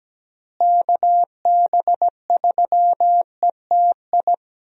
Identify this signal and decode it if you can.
Morse code, heard as a keyed tone, about 17 wpm, 700 Hz. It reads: KB3ETI
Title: USA Call Signs - 6 Characters - Repeated 3x - 17wpm